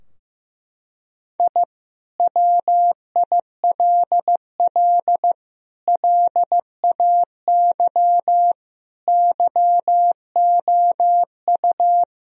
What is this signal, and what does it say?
Morse code, heard as a keyed tone, about 16 wpm, 700 Hz. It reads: I WILL LAY YOU
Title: Sentences from Top 1000 Words - Review Part 2 - 15wpm